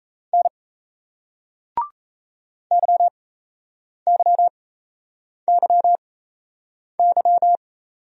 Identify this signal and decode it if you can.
Morse code, heard as a keyed tone, about 38 wpm, spaced as for 8 wpm, 700 Hz. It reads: N YYYY